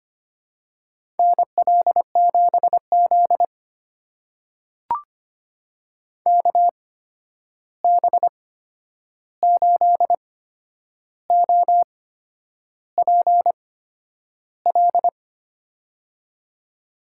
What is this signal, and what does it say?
Morse code, heard as a keyed tone, about 25 wpm, spaced as for 9 wpm, 700 Hz. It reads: NL7Z KB8OPL
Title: USA Call Signs - 8x Character Spacing - 25wpm